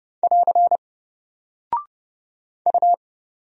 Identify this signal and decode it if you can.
Morse code, heard as a keyed tone, about 30 wpm, 700 Hz. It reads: <AR> U
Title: Single Character (Letter:Number:Punctuation:Prosign) 30wpm